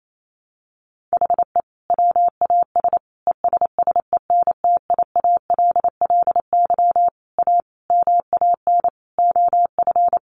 Morse code, heard as a keyed tone, about 28 wpm, 700 Hz. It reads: HE WAS ESSENTIALLY A MAN OF
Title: Sentences from Top 1300 Words 28wpm